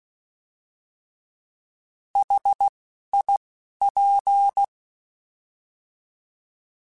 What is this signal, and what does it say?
Morse code, heard as a keyed tone, about 16 wpm, 800 Hz. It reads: H I P
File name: Contrasts / Contrasts-mp3s / 3LtrContrasts @10